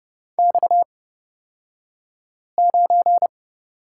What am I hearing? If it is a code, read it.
Morse code, heard as a keyed tone, about 30 wpm, spaced as for 5 wpm, 700 Hz. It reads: X9
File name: Lesson 37d - Letter x - Two Characters 30wpm